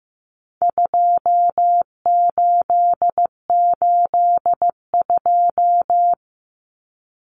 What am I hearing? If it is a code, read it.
Morse code, heard as a keyed tone, about 15 wpm, 700 Hz. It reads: 2882
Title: Common Cut Numbers - 4-Digit 15wpm